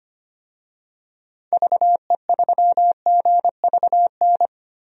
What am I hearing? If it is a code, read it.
Morse code, heard as a keyed tone, about 25 wpm, 700 Hz. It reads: VE3GVN
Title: DXCC-100 Call Signs - 6 Characters - 25wpm